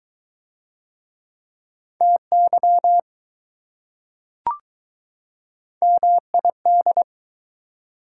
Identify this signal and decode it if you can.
Morse code, heard as a keyed tone, about 20 wpm, 700 Hz. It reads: TY MID